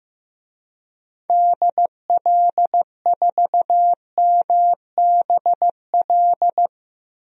Morse code, heard as a keyed tone, about 15 wpm, 700 Hz. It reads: DL4MBL
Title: DXCC-100 Call Signs - 6 Characters - 15wpm